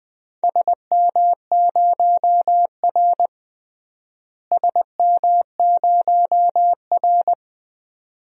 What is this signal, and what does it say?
Morse code, heard as a keyed tone, about 20 wpm, 700 Hz. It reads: SM0R SM0R
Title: DXCC-100 Call Signs - 4 Characters - Repeated 2x - 20wpm